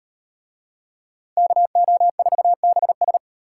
Morse code, heard as a keyed tone, about 37 wpm, 700 Hz. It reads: KO4BS